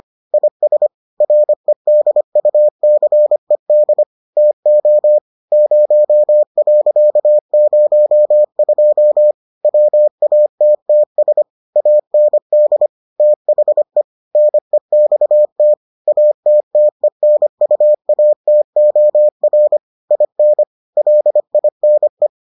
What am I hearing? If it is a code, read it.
Morse code, heard as a keyed tone, about 25 wpm, 600 Hz. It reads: IS REDUCED TO 0.02 WATTS AND THE NEXT ATTENUATOR IN LINE